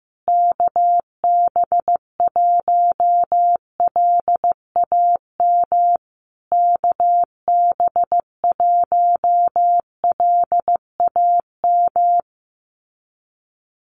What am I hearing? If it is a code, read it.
Morse code, heard as a keyed tone, about 15 wpm, 700 Hz. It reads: KB1LAM KB1LAM
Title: USA Call Signs Repeated 3x 15wpm